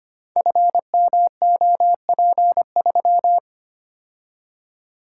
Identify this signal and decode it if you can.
Morse code, heard as a keyed tone, about 25 wpm, 700 Hz. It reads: FMOP3
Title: Code Groups - 5 Letter:Numbers - 25wpm